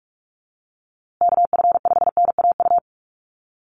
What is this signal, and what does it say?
Morse code, heard as a keyed tone, about 44 wpm, 700 Hz. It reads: KF5NAU
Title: USA Call Signs Repeated 2x 45wpm